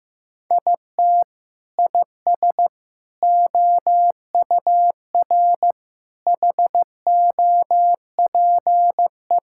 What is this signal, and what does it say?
Morse code, heard as a keyed tone, about 15 wpm, 700 Hz. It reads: IT IS OUR HOPE
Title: Sentences from Top 1000 Words - Review Part 5 - 15wpm